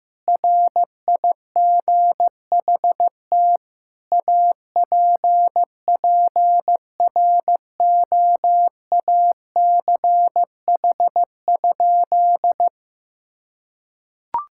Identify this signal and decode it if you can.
Morse code, heard as a keyed tone, about 15 wpm, 700 Hz. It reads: RIGHT APPROACH?
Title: Sentences from Top 1000 Words - Review Part 6 - 15wpm